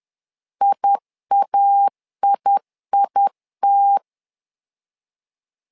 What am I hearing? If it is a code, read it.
Morse code, heard as a keyed tone, about 10 wpm, 800 Hz. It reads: IAIIT
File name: LEZIONI / 2 UHDN / 2 UH 10wpm 1714